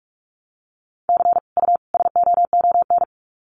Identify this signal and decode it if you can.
Morse code, heard as a keyed tone, about 44 wpm, 700 Hz. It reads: C U SOON
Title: Common QSO Phrases 45wpm